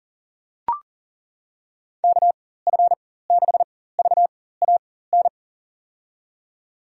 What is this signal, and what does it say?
Morse code, heard as a keyed tone, about 40 wpm, spaced as for 19 wpm, 700 Hz. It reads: KF6VAN
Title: USA Call Signs - 6 Characters - 4x Character Spacing - 40wpm